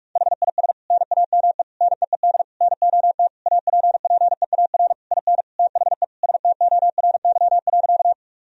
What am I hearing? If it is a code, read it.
Morse code, heard as a keyed tone, about 46 wpm, 700 Hz. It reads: HIS NAME NEED NOT APPEAR IN THE STORY.
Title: Sentences from Top 1000 Words - Review Part 2 - 45wpm